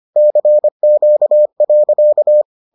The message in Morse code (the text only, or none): CQ.